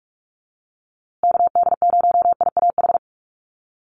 Morse code, heard as a keyed tone, about 44 wpm, 700 Hz. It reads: KD0IAH
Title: USA Call Signs Repeated 2x 45wpm